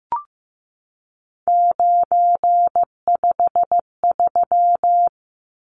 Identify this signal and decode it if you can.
Morse code, heard as a keyed tone, about 15 wpm, 700 Hz. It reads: 953
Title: Numbers - 3-Digit 15wpm